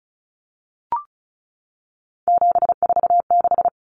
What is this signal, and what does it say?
Morse code, heard as a keyed tone, about 34 wpm, 700 Hz. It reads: 746
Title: Common Cut Numbers - 3-Digit 35wpm